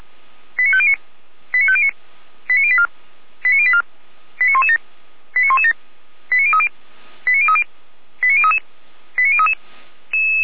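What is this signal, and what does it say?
Five-tone selective call (ZVEI), heard as quick runs of five tones, tones of about 70 ms. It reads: 89509 89509 89084 89084 89117 89117 89030 89030 89033 89033
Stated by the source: ZVEI five-tone selective call received with an SDR.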